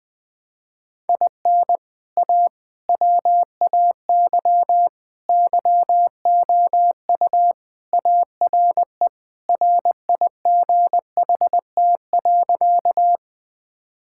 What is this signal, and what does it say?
Morse code, heard as a keyed tone, about 21 wpm, 700 Hz. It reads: IN A WAY YOU ARE RIGHT.
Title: Sentences with 3-Word Phrases - Spoken Beginning - 20wpm